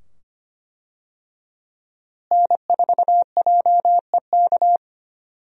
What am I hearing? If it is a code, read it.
Morse code, heard as a keyed tone, about 25 wpm, 700 Hz. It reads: N4JEK
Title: USA Call Signs - 5 Characters - Repeated 2x - 25wpm